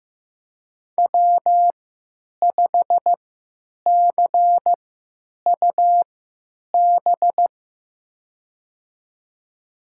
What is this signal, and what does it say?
Morse code, heard as a keyed tone, about 15 wpm, spaced as for 9 wpm, 700 Hz. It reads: W5CUB
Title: USA Call Signs - 5 Characters - 3x Character Spacing - 15wpm